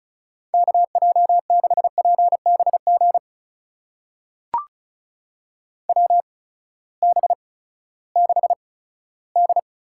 Morse code, heard as a keyed tone, about 36 wpm, spaced as for 18 wpm, 700 Hz. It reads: KJ6PBG WB6D
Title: USA Call Signs - 8x Character Spacing - 35wpm